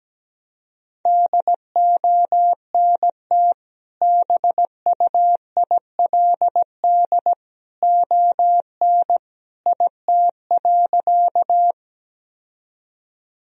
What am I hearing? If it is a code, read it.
Morse code, heard as a keyed tone, about 17 wpm, 700 Hz. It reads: DONT BUILD ON IT.
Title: Sentences from Top 1000 Words - Review Part 1 - 17wpm